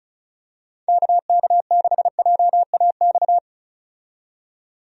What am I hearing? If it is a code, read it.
Morse code, heard as a keyed tone, about 34 wpm, 700 Hz. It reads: KK6JAX
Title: USA Call Signs - 6 Characters - 35wpm